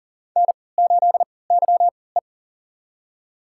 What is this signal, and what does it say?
Morse code, heard as a keyed tone, about 40 wpm, spaced as for 23 wpm, 700 Hz. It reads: N8YE
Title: USA Call Signs - 4 Characters - 3x Character Spacing - 40wpm